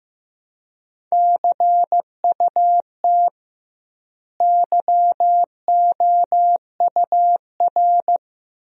Morse code, heard as a keyed tone, about 15 wpm, 700 Hz. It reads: CUT YOUR
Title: 2x Word-Spacing - Sentences from Top 300 Words - 15wpm